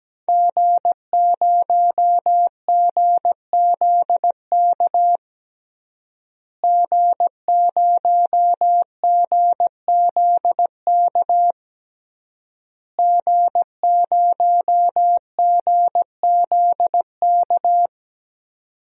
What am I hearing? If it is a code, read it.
Morse code, heard as a keyed tone, about 17 wpm, 700 Hz. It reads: G0GZK G0GZK G0GZK
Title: DXCC-100 Call Signs - 5 Characters - Repeated 3x - 17wpm